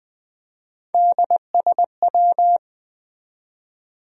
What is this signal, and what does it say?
Morse code, heard as a keyed tone, about 20 wpm, 700 Hz. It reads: DSW